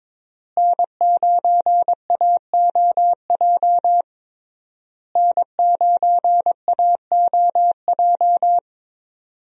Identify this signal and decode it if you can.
Morse code, heard as a keyed tone, about 21 wpm, 700 Hz. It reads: N9AOJ N9AOJ
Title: USA Call Signs - 5 Characters - Repeated 3x - 22wpm